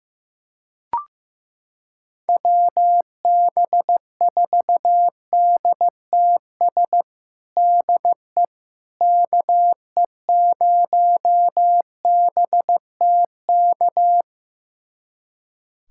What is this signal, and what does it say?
Morse code, heard as a keyed tone, about 15 wpm, 700 Hz. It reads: WB4DTS DE KE0BTK
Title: USA Callsign DE Callsign - 6 Characters - 15wpm